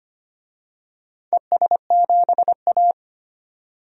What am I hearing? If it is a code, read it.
Morse code, heard as a keyed tone, about 25 wpm, 700 Hz. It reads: ES7A
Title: DXCC-100 Call Signs - 4 Characters - 25wpm